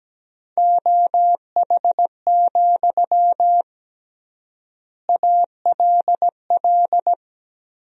Morse code, heard as a keyed tone, about 17 wpm, 700 Hz. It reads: OH, ALL
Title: 3x Word-Spacing - Sentences from Top 500 Words - 17wpm